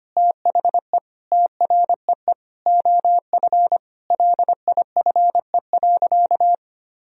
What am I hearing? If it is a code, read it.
Morse code, heard as a keyed tone, about 25 wpm, 700 Hz. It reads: THE TREE OF LIFE.
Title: Sentences from Top 1000 Words - Review Part 1 - 25wpm